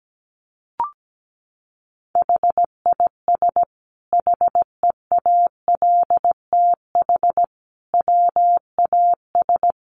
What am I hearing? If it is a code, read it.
Morse code, heard as a keyed tone, about 17 wpm, 700 Hz. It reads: HIS HEALTH WAS
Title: Sentences from Top 1400 Words 17wpm